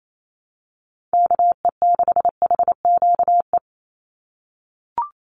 Morse code, heard as a keyed tone, about 28 wpm, 700 Hz. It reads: KE6HQE
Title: USA Call Signs 28wpm